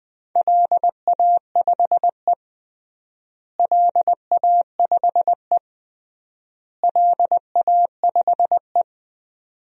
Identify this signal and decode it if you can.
Morse code, heard as a keyed tone, about 20 wpm, 700 Hz. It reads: LA5E LA5E LA5E
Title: DXCC-100 Call Signs - 4 Characters - Repeated 3x - 20wpm